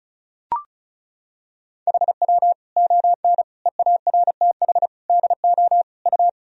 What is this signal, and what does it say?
Morse code, heard as a keyed tone, about 34 wpm, 700 Hz. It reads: HW ON EARTH DO U
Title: Sentences from Top 200 Words and Abbreviations 35wpm